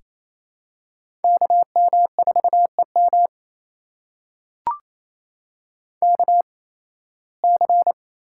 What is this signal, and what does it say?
Morse code, heard as a keyed tone, about 28 wpm, spaced as for 18 wpm, 700 Hz. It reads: KM4EM KC
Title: USA Call Signs - 8x Character Spacing - 28wpm